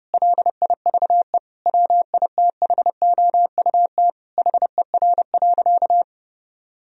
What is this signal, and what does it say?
Morse code, heard as a keyed tone, about 30 wpm, 700 Hz. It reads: LIVE WITHOUT HER.
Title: Sentences from Top 1000 Words - Review Part 3 - 30wpm